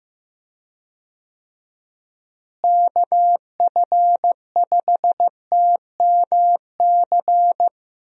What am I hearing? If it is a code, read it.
Morse code, heard as a keyed tone, about 15 wpm, 700 Hz. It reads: KF5TMC
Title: USA Call Signs - 6 Characters - Repeated 3x - 15wpm